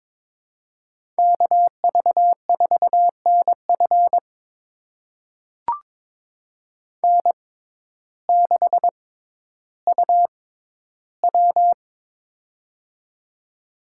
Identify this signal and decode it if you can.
Morse code, heard as a keyed tone, about 22 wpm, spaced as for 12 wpm, 700 Hz. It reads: KV4NF N6UW
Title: USA Call Signs - 6x Character Spacing - 22wpm